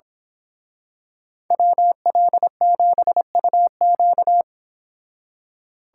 Morse code, heard as a keyed tone, about 27 wpm, 700 Hz. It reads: WL7UQ
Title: Speed Racing - USA 2x2 Call Signs 22wpm